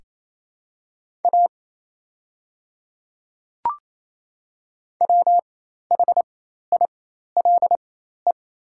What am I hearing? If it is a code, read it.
Morse code, heard as a keyed tone, about 28 wpm, spaced as for 13 wpm, 700 Hz. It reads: A WHILE
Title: 4x Character Spacing - Top 300 Words - 28wpm